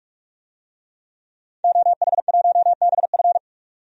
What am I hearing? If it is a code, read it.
Morse code, heard as a keyed tone, about 46 wpm, 700 Hz. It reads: OH1BF
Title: DXCC-100 Call Signs - 5 Characters - 45wpm